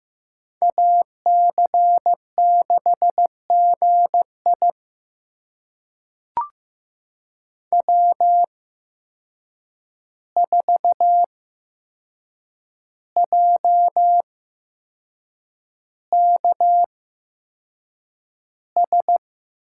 Morse code, heard as a keyed tone, about 15 wpm, spaced as for 6 wpm, 700 Hz. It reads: AC6GI W4JKS
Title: USA Call Signs - 5 Characters - 8x Character Spacing - 15wpm